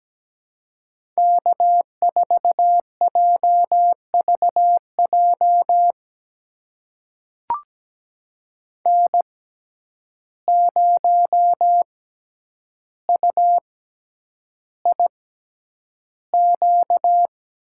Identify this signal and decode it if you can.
Morse code, heard as a keyed tone, about 17 wpm, spaced as for 9 wpm, 700 Hz. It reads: K4JVJ N0UIQ
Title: USA Call Signs - 5 Characters - 6x Character Spacing - 17wpm